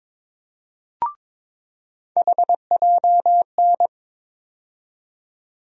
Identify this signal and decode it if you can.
Morse code, heard as a keyed tone, about 22 wpm, 700 Hz. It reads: HJN
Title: USA Call Sign Postfixes - 3 Letters 22wpm